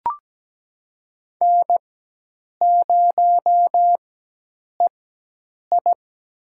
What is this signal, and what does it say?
Morse code, heard as a keyed tone, about 17 wpm, spaced as for 8 wpm, 700 Hz. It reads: N0EI
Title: USA Call Signs - 4x Character Spacing - 17wpm